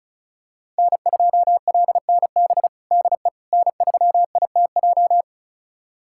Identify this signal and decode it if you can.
Morse code, heard as a keyed tone, about 36 wpm, 700 Hz. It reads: N2LNB DE N3ITJ
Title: USA Callsign DE Callsign - 5 Characters - 35wpm